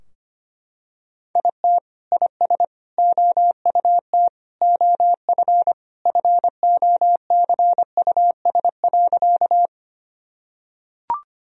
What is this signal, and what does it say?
Morse code, heard as a keyed tone, about 25 wpm, 700 Hz. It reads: IT IS OUT OF FOCUS.
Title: Sentences from Top 1000 Words - Review Part 4 - 25wpm